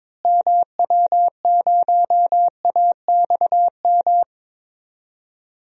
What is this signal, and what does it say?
Morse code, heard as a keyed tone, about 22 wpm, 700 Hz. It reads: MW0AXM
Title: DXCC-100 Call Signs - 6 Characters - Repeated 2x - 22wpm